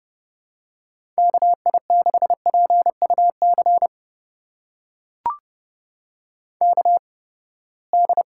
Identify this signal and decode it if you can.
Morse code, heard as a keyed tone, about 30 wpm, spaced as for 21 wpm, 700 Hz. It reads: KI6PUC KD
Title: USA Call Signs - 6 Characters - 8x Character Spacing - 30wpm